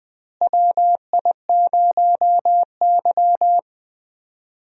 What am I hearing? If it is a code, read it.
Morse code, heard as a keyed tone, about 20 wpm, 700 Hz. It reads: WI0Y
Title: USA Call Signs - 4 Characters - Repeated 2x - 20wpm